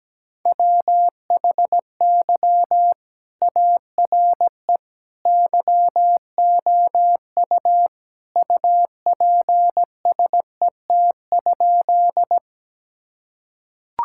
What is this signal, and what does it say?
Morse code, heard as a keyed tone, about 17 wpm, 700 Hz. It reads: WHY ARE YOU UPSET?
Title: Sentences from Top 1000 Words - Review Part 5 - 17wpm